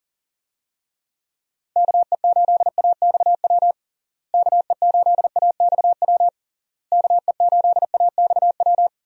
Morse code, heard as a keyed tone, about 37 wpm, 700 Hz. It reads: KE8AXW KE8AXW KE8AXW
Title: USA Call Signs - 6 Characters - Repeated 3x - 40wpm